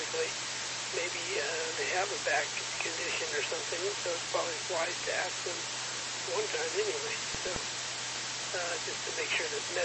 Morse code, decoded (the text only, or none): none